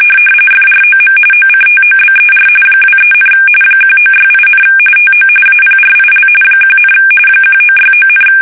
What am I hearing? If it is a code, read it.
RTTY, about 45.45 baud. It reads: FROM OCTOBER 23, 1968, TO APRIL 1, 1969,
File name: fsk850 / battleshipnj